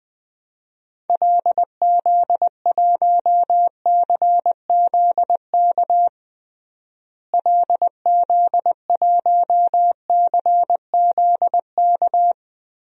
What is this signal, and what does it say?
Morse code, heard as a keyed tone, about 20 wpm, 700 Hz. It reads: LZ1CZK LZ1CZK
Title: DXCC-100 Call Signs - 6 Characters - Repeated 3x - 20wpm